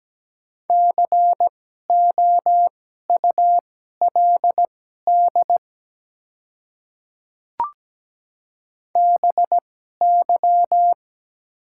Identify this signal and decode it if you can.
Morse code, heard as a keyed tone, about 17 wpm, spaced as for 12 wpm, 700 Hz. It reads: COULD BY
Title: 2x Character Spacing - Top 100 Words - 17wpm